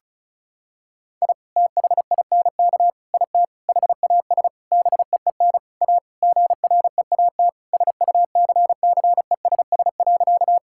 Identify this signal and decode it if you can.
Morse code, heard as a keyed tone, about 36 wpm, 700 Hz. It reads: I THINK IT HAS BEEN A GREAT SUCCESS.
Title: Sentences from Top 1000 Words - Review Part 6 - 35wpm